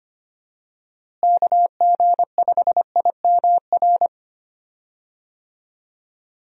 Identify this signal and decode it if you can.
Morse code, heard as a keyed tone, about 25 wpm, 700 Hz. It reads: KG5IMR
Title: USA Call Signs - 6 Characters - Repeated 2x - 25wpm